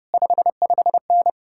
Morse code, heard as a keyed tone, about 33 wpm, 700 Hz. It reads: 55N